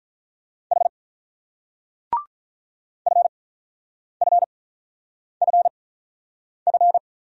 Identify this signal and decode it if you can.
Morse code, heard as a keyed tone, about 41 wpm, spaced as for 9 wpm, 700 Hz. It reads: H FFFF
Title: Speed-Racing Single Letters 35wpm